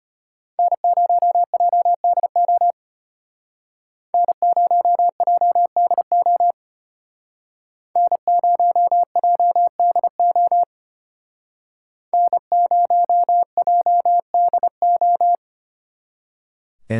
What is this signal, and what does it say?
Morse code, heard as a keyed tone, about 32 wpm, 700 Hz. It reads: N0JDO N0JDO N0JDO N0JDO